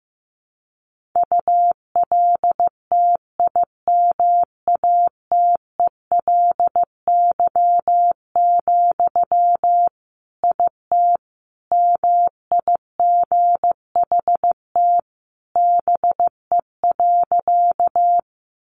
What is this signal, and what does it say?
Morse code, heard as a keyed tone, about 15 wpm, 700 Hz. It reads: ULTIMATELY, IT MIGHT BE.
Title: Sentences from Top 1200 Words 15wpm